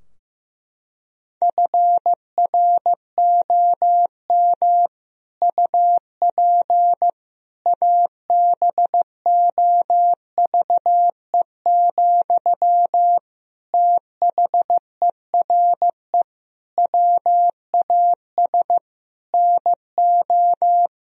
Morse code, heard as a keyed tone, about 15 wpm, 700 Hz. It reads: FROM UP ABOVE, THERE WAS NO
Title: Sentences from Top 1000 Words - Review Part 2 - 15wpm